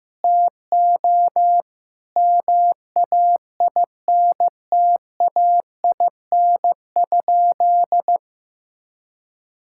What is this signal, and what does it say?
Morse code, heard as a keyed tone, about 16 wpm, 700 Hz. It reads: TO MAINTAIN?
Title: Sentences from Top 1000 Words - Review Part 5 - 15wpm